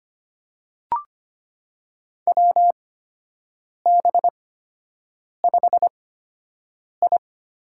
Morse code, heard as a keyed tone, about 25 wpm, spaced as for 7 wpm, 700 Hz. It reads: WB5I
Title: USA Call Signs - 8x Character Spacing - 25wpm